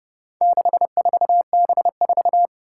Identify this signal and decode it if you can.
Morse code, heard as a keyed tone, about 30 wpm, 700 Hz. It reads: 64B4
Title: Four Challenging Letter-Number Combinations 30wpm